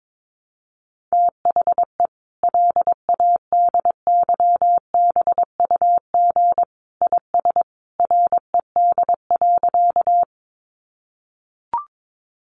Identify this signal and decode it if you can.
Morse code, heard as a keyed tone, about 22 wpm, 700 Hz. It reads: THE LADYBUG IS RED.
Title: Additional Easy Phrases 22wpm